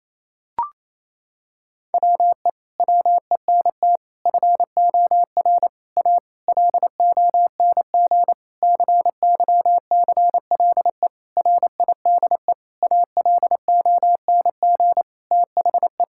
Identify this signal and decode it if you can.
Morse code, heard as a keyed tone, about 27 wpm, 700 Hz. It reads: WE WENT FOR A LONG CYCLE RIDE ALONG THE
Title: Sentences from Top 1000 Words - Review Part 6 - 28wpm